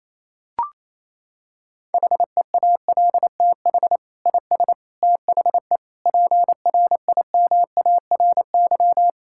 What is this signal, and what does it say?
Morse code, heard as a keyed tone, about 28 wpm, 700 Hz. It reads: HEALTH IS THE PRIMARY
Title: Sentences from Top 1000 Words - Review Part 5 - 28wpm